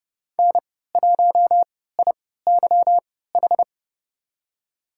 Morse code, heard as a keyed tone, about 30 wpm, spaced as for 17 wpm, 700 Hz. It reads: N1IYH
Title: USA Call Signs - 3x Character Spacing - 30wpm